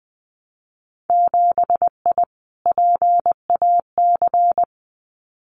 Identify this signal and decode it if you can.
Morse code, heard as a keyed tone, about 20 wpm, 700 Hz. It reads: 7I PAC
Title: Winter Field Day Exchanges Encore 20wpm